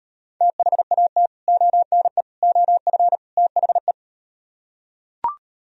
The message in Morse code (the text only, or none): THAT ONE OF THE